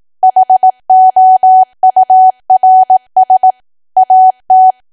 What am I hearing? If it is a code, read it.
Morse code, heard as a keyed tone, about 18 wpm, 750 Hz. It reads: HOURS AT
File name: cwa / W1AW-Level2 / 18WPM